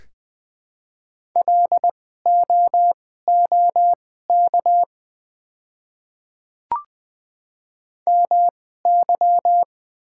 Morse code, heard as a keyed tone, about 20 wpm, spaced as for 15 wpm, 690 Hz. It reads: LOOK MY
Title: 2x Character Spacing - Top 100 Words - 20wpm